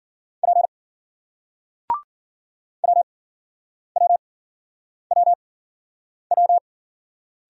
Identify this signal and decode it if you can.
Morse code, heard as a keyed tone, about 53 wpm, spaced as for 9 wpm, 700 Hz. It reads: P WWWW